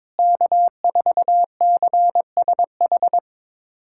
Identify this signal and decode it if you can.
Morse code, heard as a keyed tone, about 22 wpm, 700 Hz. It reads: K4CSH
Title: USA Callsign DE Callsign 22wpm